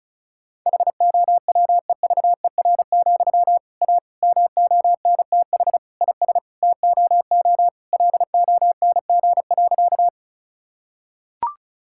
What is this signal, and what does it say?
Morse code, heard as a keyed tone, about 34 wpm, 700 Hz. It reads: HOWEVER, A MONTH IS TOO LONG.
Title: Sentences from Top 1000 Words - Review Part 5 - 35wpm